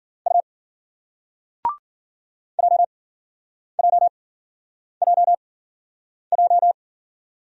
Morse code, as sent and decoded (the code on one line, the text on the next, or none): ..- / .--- .--- .--- .---
U JJJJ